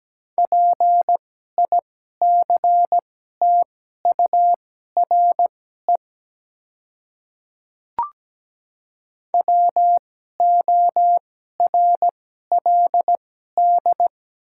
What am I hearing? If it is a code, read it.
Morse code, heard as a keyed tone, about 17 wpm, spaced as for 12 wpm, 700 Hz. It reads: PICTURE WORLD
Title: 2x Character Spacing - Top 200 Words - 17wpm